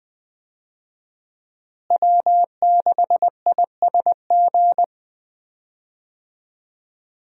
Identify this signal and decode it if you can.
Morse code, heard as a keyed tone, about 20 wpm, 700 Hz. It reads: W6ISG
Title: USA Call Signs - 5 Characters - Repeated 2x - 20wpm